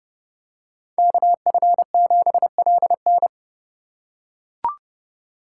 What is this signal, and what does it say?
Morse code, heard as a keyed tone, about 29 wpm, 700 Hz. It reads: KF7LN